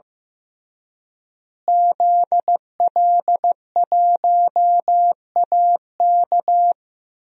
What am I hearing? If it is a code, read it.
Morse code, heard as a keyed tone, about 15 wpm, 700 Hz. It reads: ZL1AK
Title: DXCC-100 Call Signs - 5 Characters - Repeated 3x - 15wpm